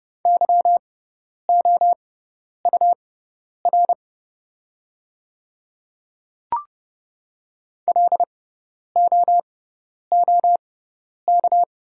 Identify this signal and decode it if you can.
Morse code, heard as a keyed tone, about 30 wpm, spaced as for 10 wpm, 700 Hz. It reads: YOUR LOOK
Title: 6x Character Spacing - Top 100 Words - 30wpm